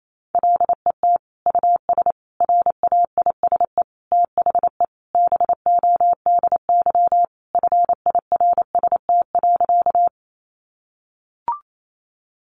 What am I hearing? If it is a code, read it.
Morse code, heard as a keyed tone, about 28 wpm, 700 Hz. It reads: LET US RAISE THE BODY FIRST.
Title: Sentences from Top 1400 Words 28wpm